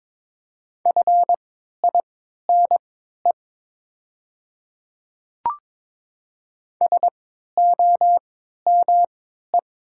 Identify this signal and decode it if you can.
Morse code, heard as a keyed tone, about 22 wpm, spaced as for 12 wpm, 690 Hz. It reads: FINE SOME
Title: 3x Character Spacing - Top 500 Words - 22wpm